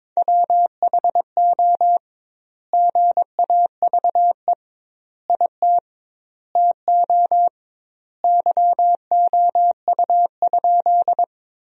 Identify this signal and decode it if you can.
Morse code, heard as a keyed tone, about 22 wpm, 700 Hz. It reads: WHO GAVE IT TO YOU?
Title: 2x Word-Spacing - Sentences from Top 500 Words - 22wpm